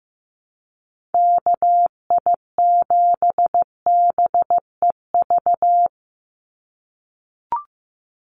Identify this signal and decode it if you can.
Morse code, heard as a keyed tone, about 15 wpm, 700 Hz. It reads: KI7BEV
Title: USA Call Signs 15wpm